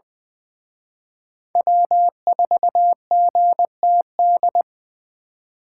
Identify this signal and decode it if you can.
Morse code, heard as a keyed tone, about 23 wpm, 700 Hz. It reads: W4GTD
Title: Speed Racing - USA 1x3 Call Signs 20wpm